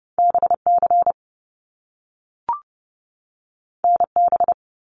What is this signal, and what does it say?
Morse code, heard as a keyed tone, about 30 wpm, 700 Hz. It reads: BC NB
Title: Canadian Provinces and Territories 30wpm